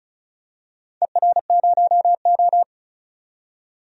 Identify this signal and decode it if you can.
Morse code, heard as a keyed tone, about 36 wpm, 700 Hz. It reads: ER0O
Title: DXCC-100 Call Signs - 4 Characters - 35wpm